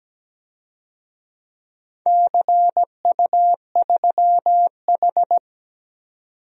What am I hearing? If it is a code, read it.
Morse code, heard as a keyed tone, about 17 wpm, 700 Hz. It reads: CU3H
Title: DXCC-100 Call Signs - 4 Characters - 17wpm